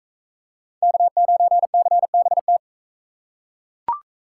Morse code, heard as a keyed tone, about 40 wpm, 700 Hz. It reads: K9CBT